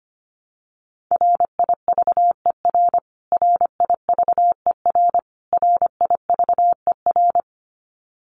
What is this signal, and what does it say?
Morse code, heard as a keyed tone, about 25 wpm, 700 Hz. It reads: RIVER RIVER RIVER
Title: Top 500 Words Repeated 25wpm